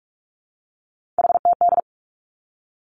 Morse code, heard as a keyed tone, about 46 wpm, 700 Hz. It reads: HTD
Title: Three Challenging Letter-Number Combinations 45wpm